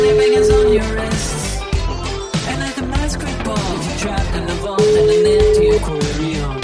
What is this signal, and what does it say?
Ringback tone (Europe, 425 Hz), with speech.